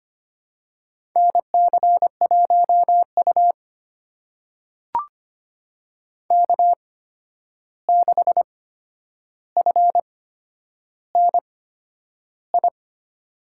Morse code, heard as a keyed tone, about 25 wpm, spaced as for 10 wpm, 700 Hz. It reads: NC1U K6FNI
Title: USA Call Signs - 8x Character Spacing - 25wpm